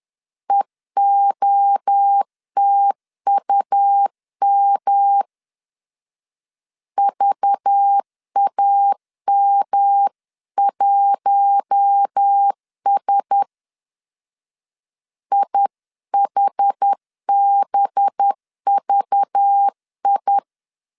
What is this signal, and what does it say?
Morse code, heard as a keyed tone, about 10 wpm, 800 Hz. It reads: EOTUM VAM1S IHBVI
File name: LEZIONI / 6 PKX0 / 6 PK 10wpm 2436